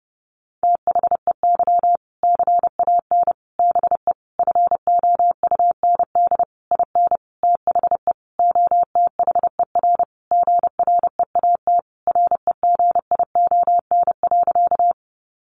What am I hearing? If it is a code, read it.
Morse code, heard as a keyed tone, about 30 wpm, 700 Hz. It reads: THEY CAN BE FOUND IN THE OTHER GREAT REGION.
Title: Sentences from Top 1500 Words 30wpm